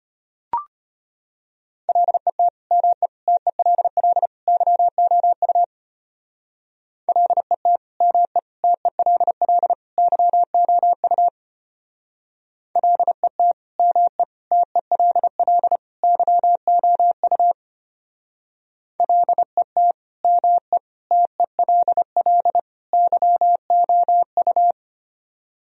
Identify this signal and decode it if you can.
Morse code, heard as a keyed tone, about 32 wpm, 700 Hz. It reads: LET ME TELL YOU LET ME TELL YOU LET ME TELL YOU LET ME TELL YOU